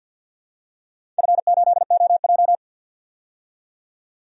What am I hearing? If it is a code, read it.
Morse code, heard as a keyed tone, about 53 wpm, 700 Hz. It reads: F8OJ
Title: DXCC-100 Call Signs - 4 Characters - 50wpm